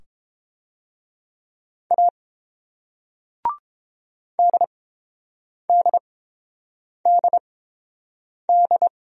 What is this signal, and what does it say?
Morse code, heard as a keyed tone, about 31 wpm, spaced as for 7 wpm, 690 Hz. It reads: A DDDD